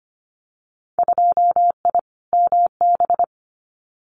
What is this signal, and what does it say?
Morse code, heard as a keyed tone, about 25 wpm, 700 Hz. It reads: 2I MB
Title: Winter Field Day Exchanges Encore 25wpm